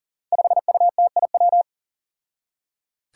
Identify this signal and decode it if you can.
Morse code, heard as a keyed tone, about 37 wpm, 700 Hz. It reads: 5UTIW
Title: Five Challenging Letter-Number Combinations 40wpm